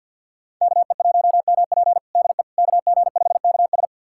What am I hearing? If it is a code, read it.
Morse code, heard as a keyed tone, about 51 wpm, 700 Hz. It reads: KE1MP DE KG5KS
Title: USA Callsign DE Callsign - 5 Characters - 50wpm